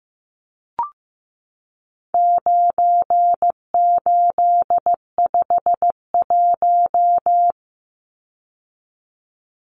Morse code, heard as a keyed tone, about 15 wpm, 700 Hz. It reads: 9851
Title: Numbers - 4-Digit 15wpm